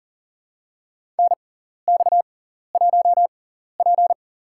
Morse code, heard as a keyed tone, about 40 wpm, spaced as for 14 wpm, 700 Hz. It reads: NX1P
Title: USA Call Signs - 4 Characters - 6x Character Spacing - 40wpm